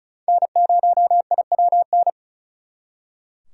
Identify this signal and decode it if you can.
Morse code, heard as a keyed tone, about 45 wpm, 700 Hz. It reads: N0IWN